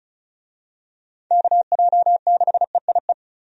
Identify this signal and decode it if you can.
Morse code, heard as a keyed tone, about 36 wpm, 700 Hz. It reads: KJ6EIE
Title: USA Call Signs - 6 Characters - 35wpm